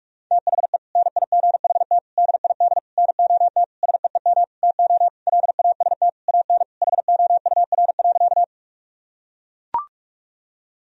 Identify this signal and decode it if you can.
Morse code, heard as a keyed tone, about 41 wpm, 700 Hz. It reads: THE NIGHT DID NOT SEEM TO LAST AN HOUR.
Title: Sentences from Top 1000 Words - Review Part 1 - 45wpm